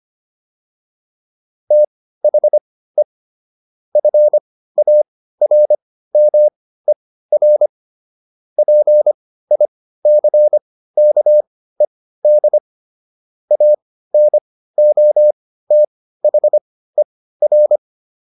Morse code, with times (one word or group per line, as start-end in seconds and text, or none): 1.7-3.0 THE
4.0-7.7 FARMER
8.6-12.6 PICKED
13.5-17.8 ANOTHER